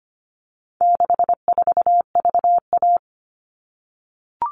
Morse code, heard as a keyed tone, about 25 wpm, 700 Hz. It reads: 64VA
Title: All Cut Numbers - 4-Digit 25wpm